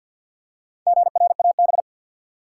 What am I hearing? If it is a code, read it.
Morse code, heard as a keyed tone, about 51 wpm, 700 Hz. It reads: GRAB
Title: Four Letter Words - 50wpm